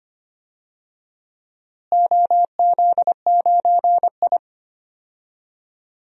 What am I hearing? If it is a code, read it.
Morse code, heard as a keyed tone, about 25 wpm, 700 Hz. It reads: OZ9I
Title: DXCC-100 Call Signs - 4 Characters - Repeated 2x - 25wpm